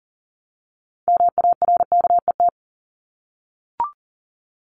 Morse code, heard as a keyed tone, about 40 wpm, 700 Hz. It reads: MARKET
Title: Top 501-750 Words 40wpm